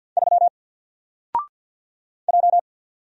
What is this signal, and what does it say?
Morse code, heard as a keyed tone, about 50 wpm, 700 Hz. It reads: 3 J
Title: Single Letter-Number 50wpm